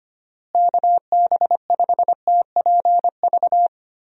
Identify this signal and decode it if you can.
Morse code, heard as a keyed tone, about 25 wpm, 700 Hz. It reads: KB5TPV
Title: USA Call Signs - 6 Characters - Repeated 3x - 25wpm